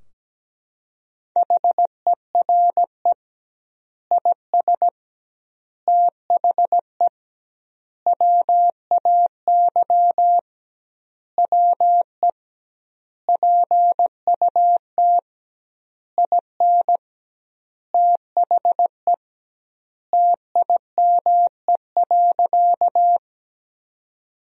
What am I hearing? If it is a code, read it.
Morse code, heard as a keyed tone, about 17 wpm, 700 Hz. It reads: HERE IS THE WAY WE PUT IN THE TIME.
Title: 2x Word-Spacing - Sentences from Top 200 Words - 17wpm